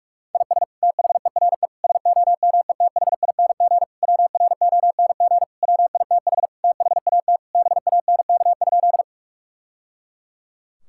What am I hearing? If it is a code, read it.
Morse code, heard as a keyed tone, about 46 wpm, 700 Hz. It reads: IS THERE SOMETHING WRONG WITH THAT BANK?
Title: Sentences from Top 1000 Words - Review Part 6 - 45wpm